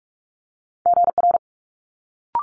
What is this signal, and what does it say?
Morse code, heard as a keyed tone, about 46 wpm, 700 Hz. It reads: GR
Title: Multi-letter Phonemes Quick Fire 45wpm